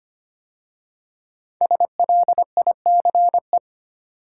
Morse code, heard as a keyed tone, about 25 wpm, 700 Hz. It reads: SLICE
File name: 5 Letter Words - First 3 Letters Given - 25wpm